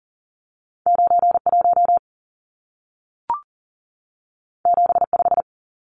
Morse code, heard as a keyed tone, about 40 wpm, 700 Hz. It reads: 91 75
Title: Numbers - 2-Digit 40wpm